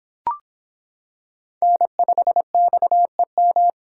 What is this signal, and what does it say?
Morse code, heard as a keyed tone, about 25 wpm, 700 Hz. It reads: N5XEM